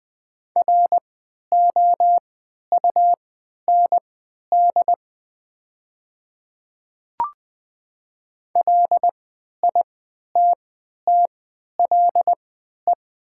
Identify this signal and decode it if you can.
Morse code, heard as a keyed tone, about 20 wpm, spaced as for 11 wpm, 700 Hz. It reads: ROUND LITTLE
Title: 3x Character Spacing - Top 300 Words - 20wpm